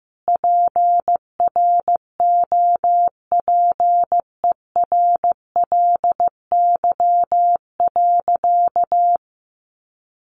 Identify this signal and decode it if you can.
Morse code, heard as a keyed tone, about 14 wpm, 700 Hz. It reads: PROPERLY.
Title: Sentences from Top 1100 Words 15wpm